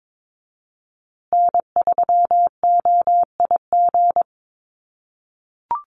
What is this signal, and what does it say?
Morse code, heard as a keyed tone, about 22 wpm, 700 Hz. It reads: N3OIG
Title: USA Call Signs Repeated 2x 22wpm